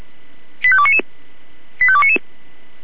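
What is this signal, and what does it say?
Five-tone selective call (ZVEI), heard as quick runs of five tones, tones of about 70 ms. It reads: 85299 85299
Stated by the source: ZVEI five-tone selective call received with SDR.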